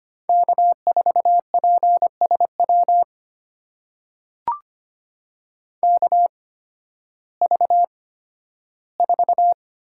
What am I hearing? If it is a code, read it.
Morse code, heard as a keyed tone, about 25 wpm, spaced as for 13 wpm, 700 Hz. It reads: K4PSW KV4
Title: USA Call Signs - 5 Characters - 8x Character Spacing - 25wpm